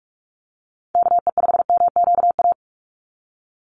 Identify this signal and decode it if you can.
Morse code, heard as a keyed tone, about 44 wpm, 700 Hz. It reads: KE5MQA
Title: USA Call Signs 45wpm